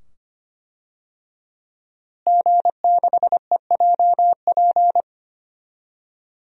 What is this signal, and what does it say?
Morse code, heard as a keyed tone, about 25 wpm, 700 Hz. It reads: G6EJP
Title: DXCC-100 Call Signs - 5 Characters - Repeated 2x - 25wpm